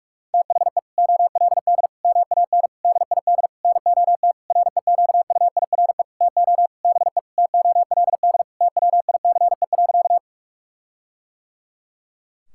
Morse code, heard as a keyed tone, about 42 wpm, 700 Hz. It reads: THE OLD MAN DID NOT REQUIRE TO BE TOLD TWICE.
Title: Sentences from Top 1600 Words 45wpm